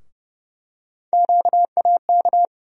Morse code, heard as a keyed tone, about 30 wpm, 700 Hz. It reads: QAK
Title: USA Call Sign Postfixes - 3 Letters 30wpm